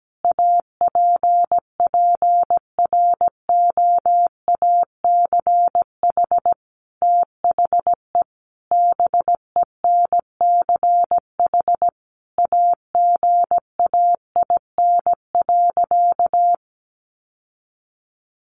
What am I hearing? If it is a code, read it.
Morse code, heard as a keyed tone, about 17 wpm, 700 Hz. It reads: APPROACH THE BENCH AGAIN.
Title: Sentences from Top 1400 Words 17wpm